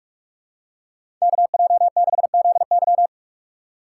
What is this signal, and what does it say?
Morse code, heard as a keyed tone, about 44 wpm, 700 Hz. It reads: KJ6ZY
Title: USA Call Signs - 3x Character Spacing - 45wpm